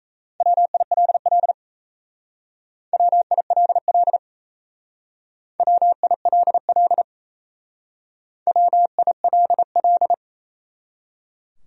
Morse code, heard as a keyed tone, about 35 wpm, 700 Hz. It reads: WILL WILL WILL WILL
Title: Speed-Racing Top 100 Words 28wpm